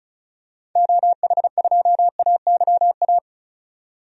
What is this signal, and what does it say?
Morse code, heard as a keyed tone, about 36 wpm, 700 Hz. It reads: OH2AYA
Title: DXCC-100 Call Signs - 6 Characters - 35wpm